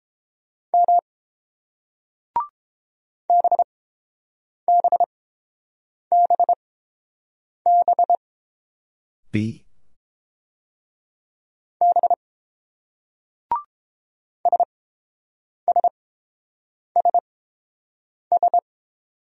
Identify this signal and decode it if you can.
Morse code, heard as a keyed tone, about 29 wpm, spaced as for 7 wpm, 700 Hz. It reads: M BBBB B SSSS